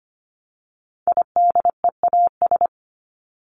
Morse code, heard as a keyed tone, about 25 wpm, 700 Hz. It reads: IDEAS
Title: ES:S Suffix - Top 100 Words with Base Words 25wpm